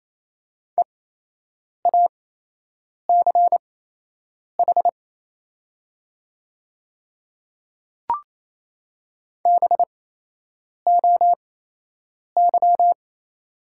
Morse code, heard as a keyed tone, about 28 wpm, spaced as for 8 wpm, 700 Hz. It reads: EACH BOY
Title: 8x Character Spacing - Top 300 Words - 28wpm